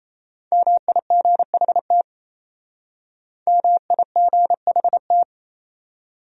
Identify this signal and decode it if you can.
Morse code, heard as a keyed tone, about 34 wpm, 700 Hz. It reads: MIGHT MIGHT
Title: Speed-Racing Top 500 Words 28wpm